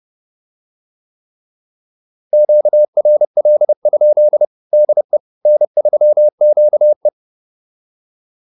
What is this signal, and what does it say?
Morse code, heard as a keyed tone, about 30 wpm, 600 Hz. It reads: QRL? DE N3QE
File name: QSO101 30